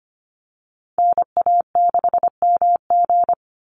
Morse code, heard as a keyed tone, about 25 wpm, 700 Hz. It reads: NA6MG
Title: NA SOTA Chaser Callsigns 25wpm